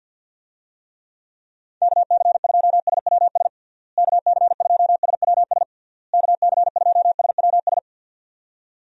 Morse code, heard as a keyed tone, about 51 wpm, 700 Hz. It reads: KC2SWS KC2SWS KC2SWS
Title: USA Call Signs - 6 Characters - Repeated 3x - 50wpm